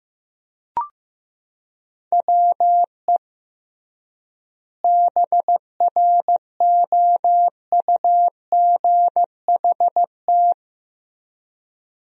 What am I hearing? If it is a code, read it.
Morse code, heard as a keyed tone, about 15 wpm, 700 Hz. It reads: WE BROUGHT
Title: 3x Word-Spacing - Sentences from Top 500 Words - 15wpm